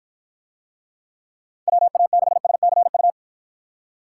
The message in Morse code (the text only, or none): WA6SCU